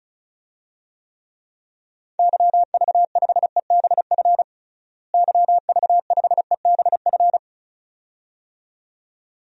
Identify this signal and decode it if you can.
Morse code, heard as a keyed tone, about 36 wpm, 700 Hz. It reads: YV5EBF YV5EBF
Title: DXCC-100 Call Signs - 6 Characters - Repeated 2x - 35wpm